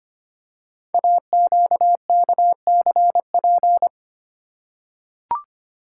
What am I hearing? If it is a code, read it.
Morse code, heard as a keyed tone, about 25 wpm, 700 Hz. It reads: AQKCP
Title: Code Groups - 5 Letters - 25wpm